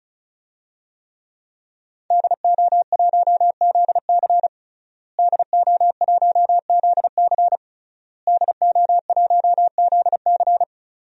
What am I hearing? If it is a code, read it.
Morse code, heard as a keyed tone, about 34 wpm, 700 Hz. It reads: DO1ZC DO1ZC DO1ZC
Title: DXCC-100 Call Signs - 5 Characters - Repeated 3x - 35wpm